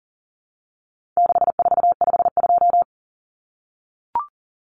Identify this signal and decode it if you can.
Morse code, heard as a keyed tone, about 40 wpm, 700 Hz. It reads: 6452
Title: Common Cut Numbers - 4-Digit 40wpm